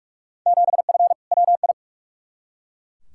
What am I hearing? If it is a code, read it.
Morse code, heard as a keyed tone, about 41 wpm, 700 Hz. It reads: 7F WI
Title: Field Day Exchanges Encore - 45wpm